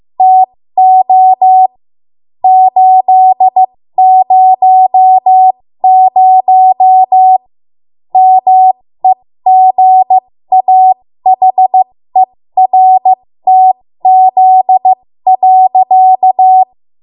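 Morse code, heard as a keyed tone, about 15 wpm, spaced as for 13 wpm, 750 Hz. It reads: TO 800 MEGAHERTZ.
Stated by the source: 13 wpm Morse code practice archive transmission.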